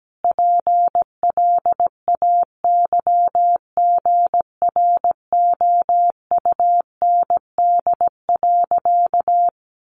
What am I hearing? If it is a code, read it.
Morse code, heard as a keyed tone, about 17 wpm, 700 Hz. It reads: PLAYGROUND.
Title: Additional Easy Phrases 17wpm